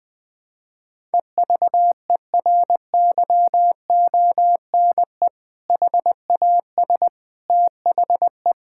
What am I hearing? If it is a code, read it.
Morse code, heard as a keyed tone, about 21 wpm, 700 Hz. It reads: EVERYONE HAS THE
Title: Sentences with 3-Word Phrases - 20wpm